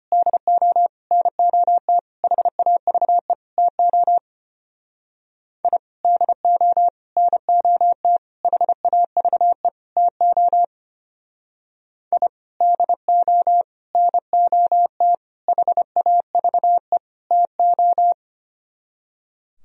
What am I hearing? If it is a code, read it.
Morse code, heard as a keyed tone, about 32 wpm, 700 Hz. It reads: DO NOT HAVE TO I DO NOT HAVE TO I DO NOT HAVE TO